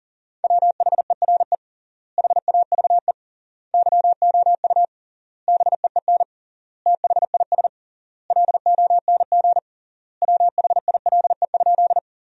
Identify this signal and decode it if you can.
Morse code, heard as a keyed tone, about 40 wpm, 700 Hz. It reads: WHERE HAVE YOU BEEN THIS LONG WHILE?
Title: 3x Word-Spacing - Sentences from Top 300 Words - 40wpm